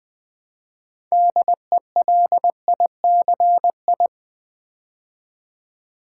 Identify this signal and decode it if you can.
Morse code, heard as a keyed tone, about 21 wpm, 700 Hz. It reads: DELICI
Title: Build A Word - 3 Letters At A Time - 9-Letter Words - 20wpm